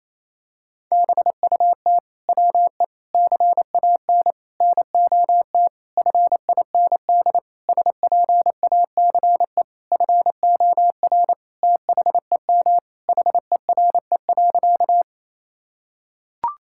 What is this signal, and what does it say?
Morse code, heard as a keyed tone, about 28 wpm, 700 Hz. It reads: BUT WE CAN NOT FIND SPACE FOR THEM HERE.
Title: Sentences from Top 1000 Words - Review Part 5 - 28wpm